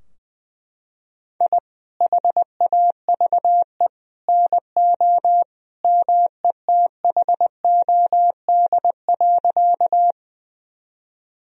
Morse code, heard as a keyed tone, about 21 wpm, 700 Hz. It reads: I HAVE NO METHOD.
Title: Sentences from Top 1000 Words - Review Part 3 - 20wpm